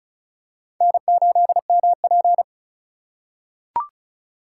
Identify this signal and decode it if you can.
Morse code, heard as a keyed tone, about 36 wpm, 700 Hz. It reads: N8MP
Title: USA Call Signs - 4 Characters - 4x Character Spacing - 35wpm